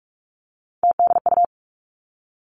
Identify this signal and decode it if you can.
Morse code, heard as a keyed tone, about 46 wpm, 700 Hz. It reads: TDU
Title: Three Challenging Letter-Number Combinations 45wpm